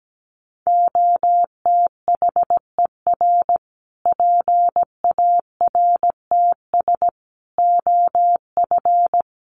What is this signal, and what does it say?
Morse code, heard as a keyed tone, about 17 wpm, 700 Hz. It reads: OTHER PARTS OF
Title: Sets of 5 Words Encore 17wpm